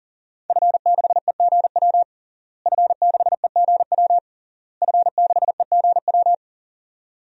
Morse code, heard as a keyed tone, about 40 wpm, 700 Hz. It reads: F6EGW F6EGW F6EGW
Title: DXCC-100 Call Signs - 5 Characters - Repeated 3x - 40wpm